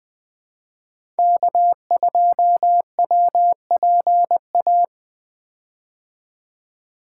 Morse code, heard as a keyed tone, about 20 wpm, 700 Hz. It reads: K2WPA
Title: 2023 POTA Activator Call Signs - 20wpm